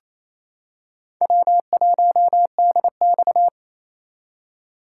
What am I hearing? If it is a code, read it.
Morse code, heard as a keyed tone, about 27 wpm, 700 Hz. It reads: W1DX
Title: K1USN SST QSO Practice - 28wpm